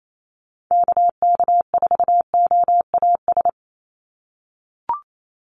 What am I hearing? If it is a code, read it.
Morse code, heard as a keyed tone, about 28 wpm, 700 Hz. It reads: KK4OAS
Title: USA Call Signs 28wpm